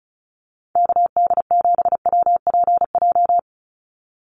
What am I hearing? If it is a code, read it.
Morse code, heard as a keyed tone, about 34 wpm, 700 Hz. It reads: KD7WPJ
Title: NA SOTA Activator Callsigns 35wpm